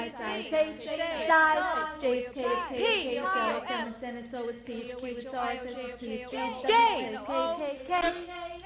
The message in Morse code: none

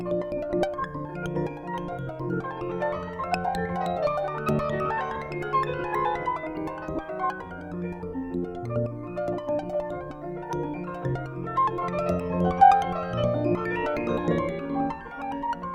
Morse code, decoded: none

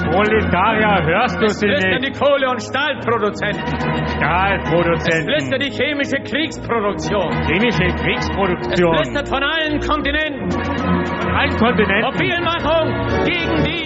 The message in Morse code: none

